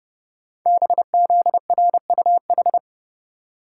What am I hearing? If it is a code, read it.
Morse code, heard as a keyed tone, about 29 wpm, 700 Hz. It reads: BZRUH
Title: Code Groups - 5 Letters - 30wpm